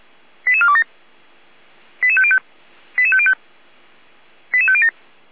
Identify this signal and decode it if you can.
Five-tone selective call (ZVEI), heard as quick runs of five tones, tones of about 70 ms. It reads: 88427 88505 88505 88507